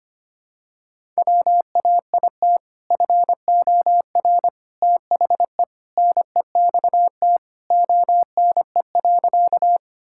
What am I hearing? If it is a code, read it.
Morse code, heard as a keyed tone, about 25 wpm, 700 Hz. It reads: WAIT FOR THE NEXT ONE.
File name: practice / 25wpm / Sentences with 2-Word Phrases - 25wpm